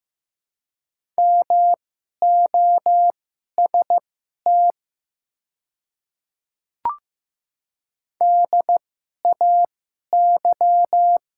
Morse code, heard as a keyed tone, about 15 wpm, spaced as for 11 wpm, 700 Hz. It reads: MOST DAY
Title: 2x Character Spacing - Top 100 Words - 15wpm